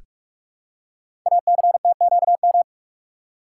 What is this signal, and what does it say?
Morse code, heard as a keyed tone, about 44 wpm, 690 Hz. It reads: ACTQM
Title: Code Groups - 5 Letters - 45wpm